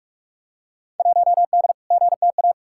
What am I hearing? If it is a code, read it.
Morse code, heard as a keyed tone, about 47 wpm, 700 Hz. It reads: 1D GTA
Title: Field Day Exchanges - 45wpm